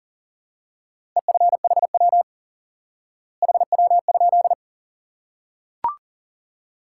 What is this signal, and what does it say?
Morse code, heard as a keyed tone, about 37 wpm, 700 Hz. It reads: EFHW HW?
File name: IQE Course 3 - Lesson 89 - YAGI - Two Elements - 40wpm